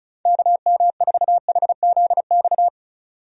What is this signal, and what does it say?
Morse code, heard as a keyed tone, about 34 wpm, 700 Hz. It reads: KM4HZX
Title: USA Call Signs - 6 Characters - 35wpm